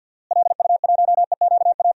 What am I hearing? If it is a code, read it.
Morse code, heard as a keyed tone, about 52 wpm, 700 Hz. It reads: LU1EQA
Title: DXCC-100 Call Signs - 6 Characters - Repeated 3x - 50wpm